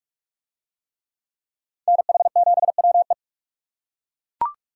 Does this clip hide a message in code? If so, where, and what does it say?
Yes: Morse code from 1.9 to 3.1 s, heard as a keyed tone, about 41 wpm, 700 Hz. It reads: NH7WE